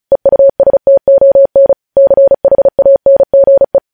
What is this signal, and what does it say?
Morse code, heard as a keyed tone, about 36 wpm, 550 Hz. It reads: EUSTON CHANGE